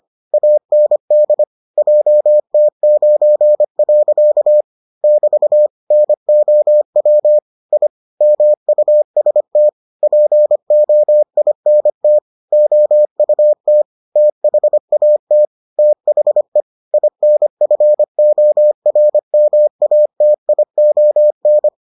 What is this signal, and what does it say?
Morse code, heard as a keyed tone, about 25 wpm, 600 Hz. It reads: AND JT9. <BT> NOW I MUST POINT OUT THAT THE INFORMATION